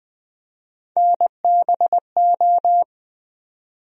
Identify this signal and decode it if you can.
Morse code, heard as a keyed tone, about 21 wpm, 700 Hz. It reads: NBO
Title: USA Call Sign Postfixes - 3 Letters 20wpm